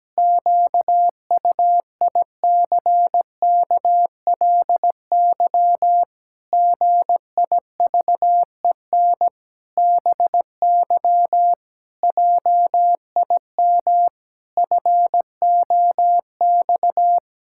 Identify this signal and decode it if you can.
Morse code, heard as a keyed tone, about 17 wpm, 700 Hz. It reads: QUICKLY GIVEN BY JIM FOX
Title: Pangram Challenge v2 - 17wpm